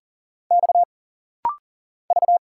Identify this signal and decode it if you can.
Morse code, heard as a keyed tone, about 40 wpm, 700 Hz. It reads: X V
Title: Single Letters - Rapid-Fire 40wpm